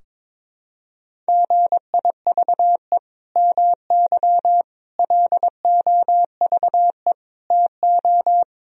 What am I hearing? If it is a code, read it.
Morse code, heard as a keyed tone, about 21 wpm, 700 Hz. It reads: GIVE MY LOVE TO
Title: Sentences from Top 1000 Words - Review Part 4 - 22wpm